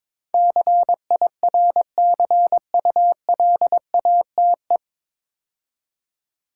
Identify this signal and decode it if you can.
Morse code, heard as a keyed tone, about 22 wpm, 700 Hz. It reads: CIRCULATE
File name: 9 Letter Words - First 3 Letters Given - 22wpm